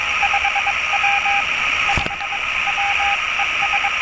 Morse code, heard as a keyed tone, about 22 wpm, spaced as for 19 wpm, 800 Hz, with radio noise. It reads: 5W 5W ES
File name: attachments / 22.07.2017 RV3GM UA1CEG RX SPUTNIK